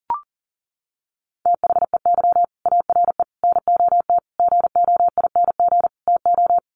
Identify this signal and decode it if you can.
Morse code, heard as a keyed tone, about 40 wpm, 700 Hz. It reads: THEY ARE NOT GOING TO
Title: Sets of 5 Words Encore2 40wpm